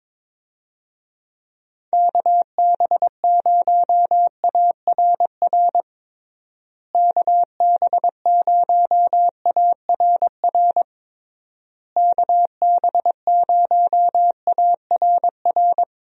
Morse code, heard as a keyed tone, about 22 wpm, 700 Hz. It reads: KB0ARR KB0ARR KB0ARR
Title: USA Call Signs - 6 Characters - Repeated 3x - 22wpm